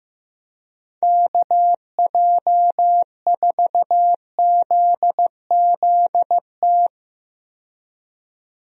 Morse code, heard as a keyed tone, about 15 wpm, 700 Hz. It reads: KJ4ZZT
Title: USA Call Signs - 6 Characters - 15wpm